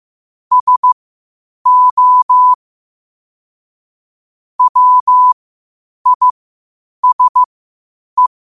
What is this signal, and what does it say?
Morse code, heard as a keyed tone, about 15 wpm, spaced as for 8 wpm, 1000 Hz. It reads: SO WISE
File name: qotdfiles / 8wpm